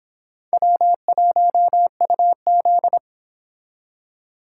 Morse code, heard as a keyed tone, about 26 wpm, 700 Hz. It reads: W1UZ